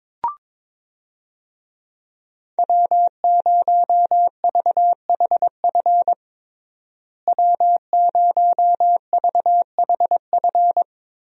Morse code, heard as a keyed tone, about 21 wpm, 700 Hz. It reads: W0VHF W0VHF
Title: USA Call Signs - 5 Characters - Repeated 3x - 22wpm